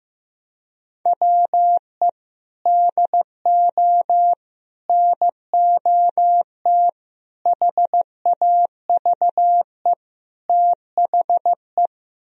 Morse code, heard as a keyed tone, about 15 wpm, 700 Hz. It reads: WE DO NOT HAVE THE
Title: Speed-Racing - Sets of 5 Words 15wpm